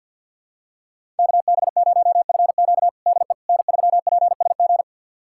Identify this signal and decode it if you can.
Morse code, heard as a keyed tone, about 47 wpm, 700 Hz. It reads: KB0FQ DE N3PSG
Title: USA Callsign DE Callsign - 5 Characters - 50wpm